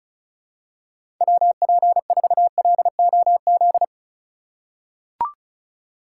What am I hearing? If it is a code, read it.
Morse code, heard as a keyed tone, about 36 wpm, 700 Hz. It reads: WP4LOZ
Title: USA Call Signs - 6 Characters - 2x Character Spacing - 35wpm